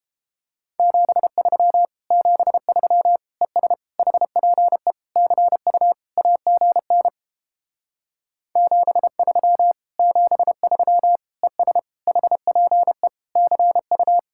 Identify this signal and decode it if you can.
Morse code, heard as a keyed tone, about 31 wpm, 700 Hz. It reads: 73 73 ES HPE CU AGN 73 73 ES HPE CU